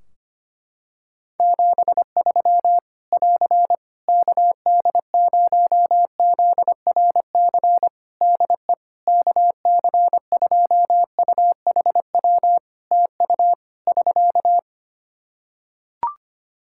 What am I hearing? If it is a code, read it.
Morse code, heard as a keyed tone, about 25 wpm, 700 Hz. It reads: 73 <AR> KD0ZRC DE KC2UHW TU <SK>
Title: Practice QSO - Exchange 4 - Ending - 25wpm